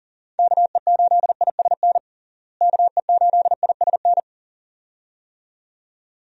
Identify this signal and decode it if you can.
Morse code, heard as a keyed tone, about 37 wpm, 700 Hz. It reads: KE8ISN KE8ISN
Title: USA Call Signs - 6 Characters - Repeated 2x - 40wpm